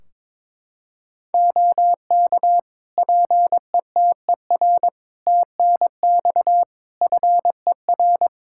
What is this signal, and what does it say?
Morse code, heard as a keyed tone, about 22 wpm, 700 Hz. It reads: OK PETER TNX FER
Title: K1USN SST QSO Practice 22wpm